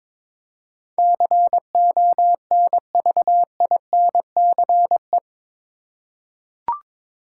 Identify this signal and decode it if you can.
Morse code, heard as a keyed tone, about 22 wpm, 700 Hz. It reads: CONVINCE
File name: practice / 22wpm / Eight Letter Words - 22wpm